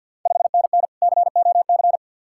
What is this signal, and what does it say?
Morse code, heard as a keyed tone, about 48 wpm, 700 Hz. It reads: 5NN COC
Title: Ontario QSO Party Exchange 50wpm